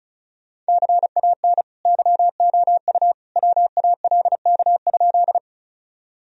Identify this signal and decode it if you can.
Morse code, heard as a keyed tone, about 36 wpm, 700 Hz. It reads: CAN YOU WALK?
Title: Sentences from Top 1000 Words - Review Part 1 - 35wpm